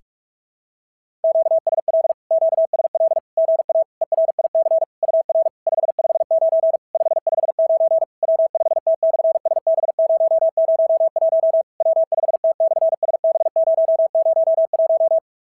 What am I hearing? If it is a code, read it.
Morse code, heard as a keyed tone, about 44 wpm, 660 Hz. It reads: QSL QSL GA ERIC UR 559 559 W5T/SB001 W5T/SB001
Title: SOTA QSO Practice 45wpm